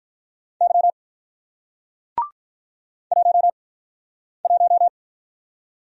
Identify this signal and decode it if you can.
Morse code, heard as a keyed tone, about 48 wpm, spaced as for 9 wpm, 700 Hz. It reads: Y 11